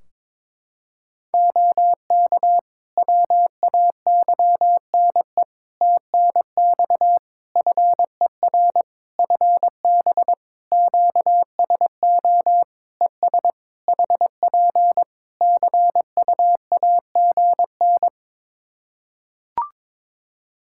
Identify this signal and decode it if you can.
Morse code, heard as a keyed tone, about 22 wpm, 700 Hz. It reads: OK WAYNE TNX FER FB QSO ES HP CUAGN
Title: Practice QSO - Exchange 4 - Ending - 22wpm